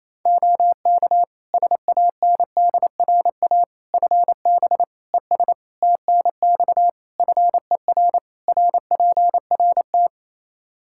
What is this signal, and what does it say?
Morse code, heard as a keyed tone, about 28 wpm, 700 Hz. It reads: OK SANDRA FB ES TNX FER RPRT
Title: Practice QSO - Exchange 2 - Info - 28wpm